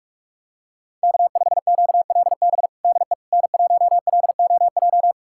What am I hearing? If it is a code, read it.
Morse code, heard as a keyed tone, about 46 wpm, 700 Hz. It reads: K5QLB DE N1LOJ
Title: USA Callsign DE Callsign - 5 Characters - 45wpm